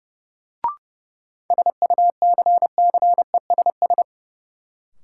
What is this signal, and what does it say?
Morse code, heard as a keyed tone, about 30 wpm, 700 Hz. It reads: SUCCESS
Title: Sentences from Top 900 Words - New Word Drill - 30wpm